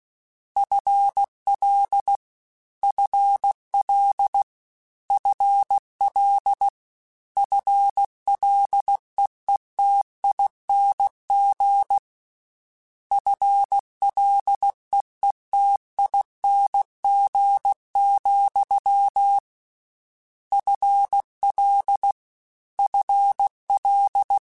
Morse code, heard as a keyed tone, about 16 wpm, 800 Hz. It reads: FL FL FL FLEETING FLEETING, FL FL
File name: Combinations / Combinations-mp3s / Combinations@15